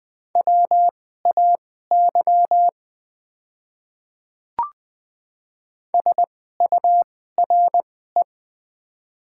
Morse code, heard as a keyed tone, about 20 wpm, spaced as for 15 wpm, 700 Hz. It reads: WAY SURE
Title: 2x Character Spacing - Top 300 Words - 20wpm